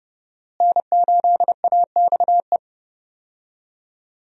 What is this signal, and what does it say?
Morse code, heard as a keyed tone, about 31 wpm, 700 Hz. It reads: N8AXE
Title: USA Call Signs - 5 Characters - Repeated 3x - 30wpm